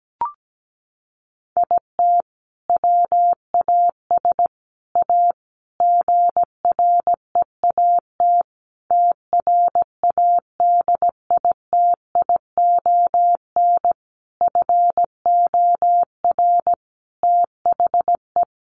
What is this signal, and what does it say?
Morse code, heard as a keyed tone, about 17 wpm, 700 Hz. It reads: IT WAS A GREAT TRADITION FOR THE
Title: Sentences from Top 1500 Words 17wpm